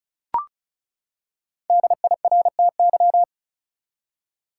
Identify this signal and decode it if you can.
Morse code, heard as a keyed tone, about 36 wpm, 700 Hz. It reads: DIRTY
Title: Five Letter Words - 35wpm